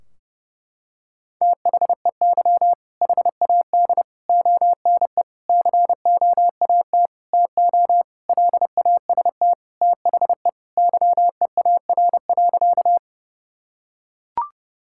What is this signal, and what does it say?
Morse code, heard as a keyed tone, about 30 wpm, 700 Hz. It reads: THEY HAD ONE COAT TO LAST THE YEAR.
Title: Sentences from Top 1600 Words 30wpm